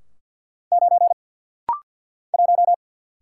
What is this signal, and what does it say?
Morse code, heard as a keyed tone, about 50 wpm, 700 Hz. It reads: 9 1
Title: Single Letter-Number Alt - Rapid-Fire 50wpm